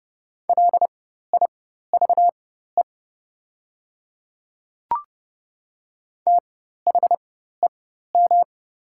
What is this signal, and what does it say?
Morse code, heard as a keyed tone, about 31 wpm, spaced as for 14 wpm, 700 Hz. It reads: LIVE THEM
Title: 4x Character Spacing - Top 200 Words - 30wpm